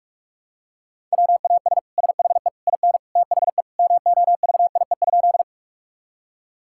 Morse code, heard as a keyed tone, about 46 wpm, 700 Hz. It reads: WAS SHE IN THE MOVIE?
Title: Sentences from Top 1000 Words - Review Part 5 - 45wpm